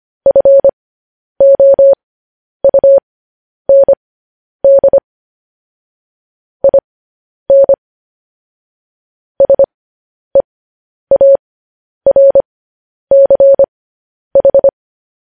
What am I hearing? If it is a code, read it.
Morse code, heard as a keyed tone, about 25 wpm, spaced as for 10 wpm, 550 Hz. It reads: FOUND IN SEARCH
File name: podcast / News headlines-1x-25x10